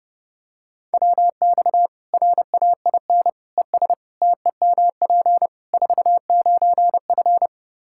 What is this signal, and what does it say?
Morse code, heard as a keyed tone, about 30 wpm, 700 Hz. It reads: WX RAIN ES TEMP 49F
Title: Practice QSO - Exchange 2 - Info - 30wpm